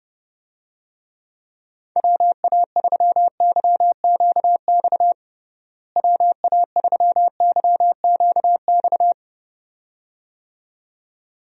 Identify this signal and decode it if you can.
Morse code, heard as a keyed tone, about 30 wpm, 700 Hz. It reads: WA3YQX WA3YQX
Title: USA Call Signs - 6 Characters - Repeated 2x - 30wpm